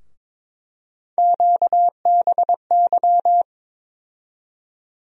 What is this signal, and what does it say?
Morse code, heard as a keyed tone, about 21 wpm, 700 Hz. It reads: QBY
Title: USA Call Sign Postfixes - 3 Letters 22wpm